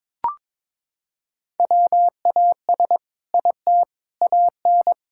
Morse code, heard as a keyed tone, about 22 wpm, 700 Hz. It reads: WAS IT AN
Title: Sentences from Top 1000 Words - Review Part 6 - 22wpm